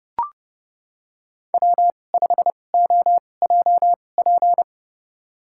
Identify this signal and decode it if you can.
Morse code, heard as a keyed tone, about 30 wpm, spaced as for 22 wpm, 700 Hz. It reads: W5OJP
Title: USA Call Signs - 5 Characters - 2x Character Spacing - 30wpm